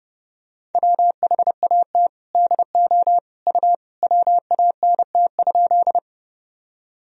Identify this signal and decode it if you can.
Morse code, heard as a keyed tone, about 32 wpm, 700 Hz. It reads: WHAT DO U WANT?
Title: Sentences from Top 200 Words and Abbreviations 30wpm